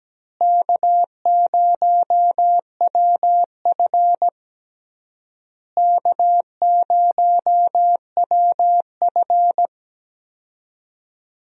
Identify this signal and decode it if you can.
Morse code, heard as a keyed tone, about 17 wpm, 700 Hz. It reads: K0WF K0WF
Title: USA Call Signs - 4 Characters - Repeated 3x - 17wpm